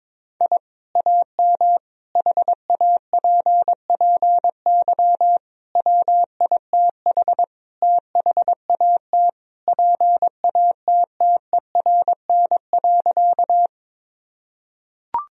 Morse code, heard as a keyed tone, about 22 wpm, 700 Hz. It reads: I AM HAPPY WITH THAT PATTERN.
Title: Sentences from Top 1000 Words - Review Part 2 - 22wpm